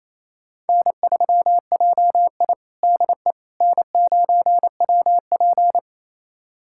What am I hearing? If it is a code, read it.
Morse code, heard as a keyed tone, about 27 wpm, 700 Hz. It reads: N3JI DE N9WP
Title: USA Callsign DE Callsign - 4 Characters - 28wpm